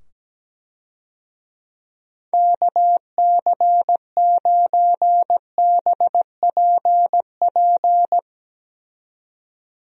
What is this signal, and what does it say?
Morse code, heard as a keyed tone, about 17 wpm, 700 Hz. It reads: KC9BPP
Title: USA Call Signs - 6 Characters - 17wpm